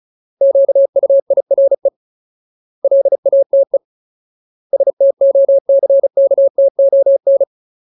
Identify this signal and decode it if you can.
Morse code, heard as a keyed tone, about 34 wpm, 550 Hz. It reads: QUIRE LATE STOCKTON
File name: Ultimate Word List / 35WPM Ultimate